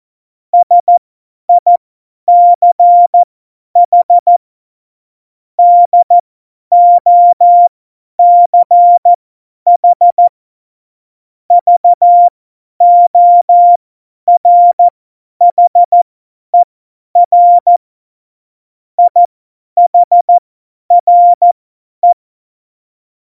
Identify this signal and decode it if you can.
Morse code, heard as a keyed tone, about 14 wpm, spaced as for 10 wpm, 700 Hz. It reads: SICH DOCH VORHER IHRE
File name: Morsekurs / Witze / Witz 09. April 2019, 50 F70